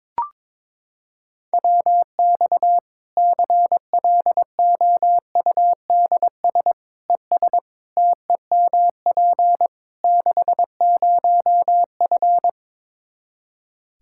Morse code, heard as a keyed tone, about 22 wpm, 700 Hz. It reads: WX CLOUDS ES TEMP 60F
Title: Practice QSO - Exchange 2 - Info - 22wpm